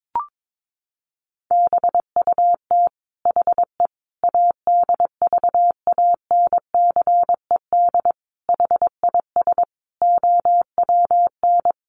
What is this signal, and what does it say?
Morse code, heard as a keyed tone, about 22 wpm, 700 Hz. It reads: BUT HE ADVANCED HIS OWN
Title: Sentences from Top 1200 Words 22wpm